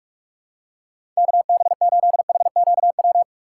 Morse code, heard as a keyed tone, about 43 wpm, 700 Hz. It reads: KB8HQW
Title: USA Call Signs - 6 Characters - 4x Character Spacing - 45wpm